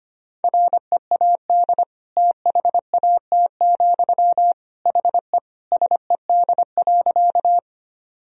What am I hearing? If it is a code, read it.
Morse code, heard as a keyed tone, about 25 wpm, 700 Hz. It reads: READ THAT, HE SED.
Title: Sentences from Top 200 Words and Abbreviations 25wpm